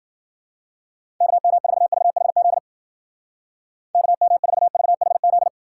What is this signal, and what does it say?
Morse code, heard as a keyed tone, about 50 wpm, 700 Hz. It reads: KM4VHZ KM4VHZ
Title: Speed Racing - USA 2x3 Call Signs 40wpm